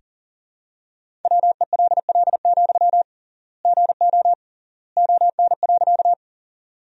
Morse code, heard as a keyed tone, about 40 wpm, 700 Hz. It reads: WELL, GO ON.
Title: 3x Word-Spacing - Sentences from Top 200 Words - 40wpm